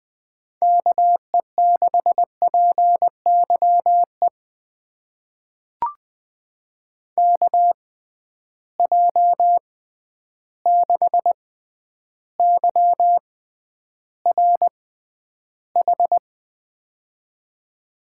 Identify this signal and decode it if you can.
Morse code, heard as a keyed tone, about 20 wpm, spaced as for 10 wpm, 700 Hz. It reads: KE6PYE KJ6YRH
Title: USA Call Signs - 6 Characters - 6x Character Spacing - 20wpm